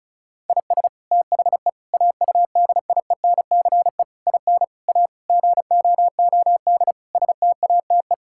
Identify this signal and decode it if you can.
Morse code, heard as a keyed tone, about 34 wpm, 700 Hz. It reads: IS THE AUDIENCE IN A GOOD STATE
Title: Sentences from Top 1000 Words - Review Part 5 - 35wpm